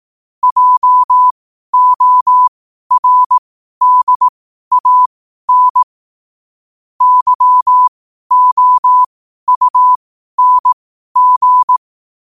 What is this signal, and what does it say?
Morse code, heard as a keyed tone, about 18 wpm, spaced as for 12 wpm, 1000 Hz. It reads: JORDAN YOUNG
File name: qotdfiles / 12wpm